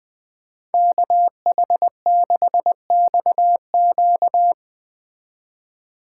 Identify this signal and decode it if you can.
Morse code, heard as a keyed tone, about 19 wpm, 700 Hz. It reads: KH6XQ
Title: Speed Racing - USA Call Signs 17wpm